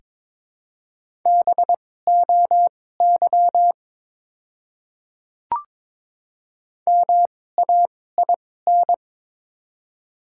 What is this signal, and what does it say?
Morse code, heard as a keyed tone, about 23 wpm, spaced as for 16 wpm, 700 Hz. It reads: BOY MAIN
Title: 2x Character Spacing - Top 300 Words - 22wpm